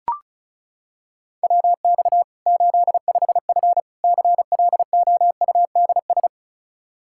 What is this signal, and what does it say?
Morse code, heard as a keyed tone, about 34 wpm, 700 Hz. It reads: WX 85F CLOUDS
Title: Practice QSO - Exchange 2 - Info - 35wpm